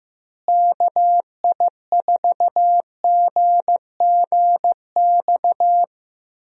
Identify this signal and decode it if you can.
Morse code, heard as a keyed tone, about 15 wpm, 700 Hz. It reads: KI4GGX
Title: USA Call Signs - 6 Characters - Repeated 3x - 15wpm